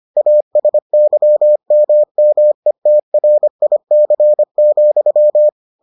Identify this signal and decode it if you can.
Morse code, heard as a keyed tone, about 25 wpm, 600 Hz. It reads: ASYMMETRIC,